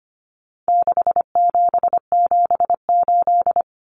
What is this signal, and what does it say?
Morse code, heard as a keyed tone, about 25 wpm, 700 Hz. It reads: 6778
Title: Numbers - 4-Digit 25wpm